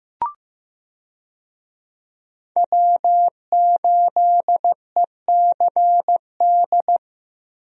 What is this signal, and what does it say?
Morse code, heard as a keyed tone, about 15 wpm, 700 Hz. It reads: W8ECD
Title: USA Call Signs - 5 Characters - Repeated 3x - 15wpm